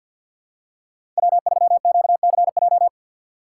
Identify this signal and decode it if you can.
Morse code, heard as a keyed tone, about 50 wpm, 690 Hz. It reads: W3QCJ
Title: USA Call Signs - 5 Characters - 4x Character Spacing - 50wpm